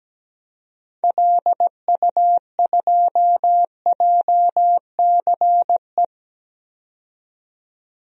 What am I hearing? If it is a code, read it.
Morse code, heard as a keyed tone, about 17 wpm, 700 Hz. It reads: LU2JCE
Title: DXCC-100 Call Signs - 6 Characters - 17wpm